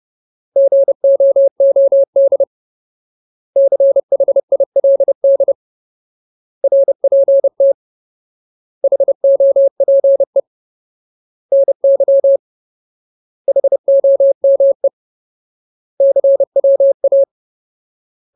Morse code, heard as a keyed tone, about 30 wpm, 550 Hz. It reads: GOOD CHILD RPT HOPE NY HOME CWA
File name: Ultimate Word List / 30WPM Ultimate